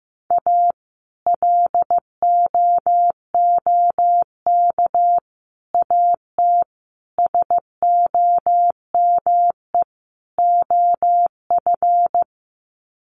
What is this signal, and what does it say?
Morse code, heard as a keyed tone, about 15 wpm, 700 Hz. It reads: A LOOK AT SOME OF
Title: Sets of 5 Words Encore2 15wpm